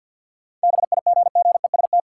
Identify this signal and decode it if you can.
Morse code, heard as a keyed tone, about 50 wpm, 700 Hz. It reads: BIGGEST
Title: Sets of 4 Words - New Word Drill - 50wpm